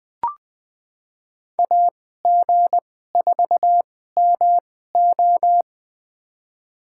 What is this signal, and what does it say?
Morse code, heard as a keyed tone, about 21 wpm, spaced as for 15 wpm, 700 Hz. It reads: AG4MO
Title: USA Call Signs - 5 Characters - 2x Character Spacing - 20wpm